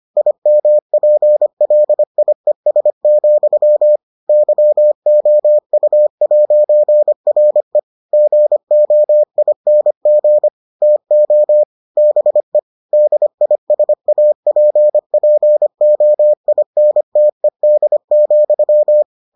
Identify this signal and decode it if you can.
Morse code, heard as a keyed tone, about 25 wpm, 600 Hz. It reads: IMPLIES, YOU'RE GOING TO BE DISAPPOINTED,